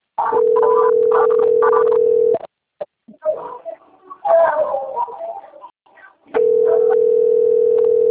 Ringback tone (North America), 440 + 480 Hz.